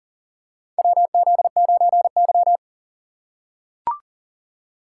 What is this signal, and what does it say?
Morse code, heard as a keyed tone, about 40 wpm, 700 Hz. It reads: WZ9Y
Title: USA Call Signs - 4 Characters - 4x Character Spacing - 40wpm